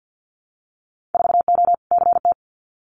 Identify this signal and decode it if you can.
Morse code, heard as a keyed tone, about 50 wpm, 700 Hz. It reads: 4O CT
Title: Winter Field Day Exchanges Encore 50wpm